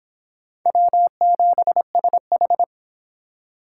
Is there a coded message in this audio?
Yes: Morse code from 0.7 to 2.6 s, heard as a keyed tone, about 25 wpm, 700 Hz. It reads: W7SH